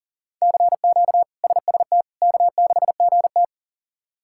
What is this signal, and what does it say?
Morse code, heard as a keyed tone, about 37 wpm, 700 Hz. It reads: CQ SST K6GT
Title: K1USN SST QSO Practice - 40wpm